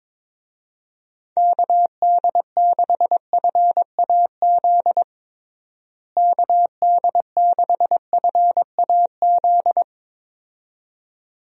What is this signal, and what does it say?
Morse code, heard as a keyed tone, about 22 wpm, 700 Hz. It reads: KD6FAZ KD6FAZ
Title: USA Call Signs - 6 Characters - Repeated 2x - 22wpm